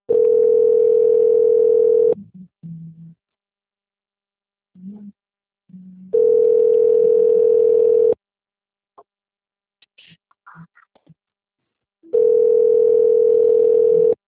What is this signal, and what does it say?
Ringback tone (North America), 440 + 480 Hz.